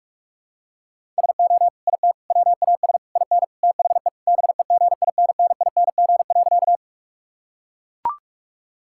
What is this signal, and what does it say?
Morse code, heard as a keyed tone, about 42 wpm, 700 Hz. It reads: SO IT WAS IN THE BEGINNING.
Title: Sentences from Top 1000 Words - Review Part 5 - 45wpm